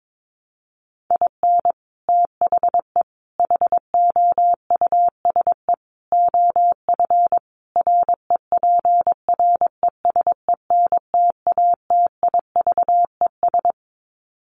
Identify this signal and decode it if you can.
Morse code, heard as a keyed tone, about 22 wpm, 700 Hz. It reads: IN THE HOUSE OF REPRESENTATIVES
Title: Sets of 5 Words Encore 22wpm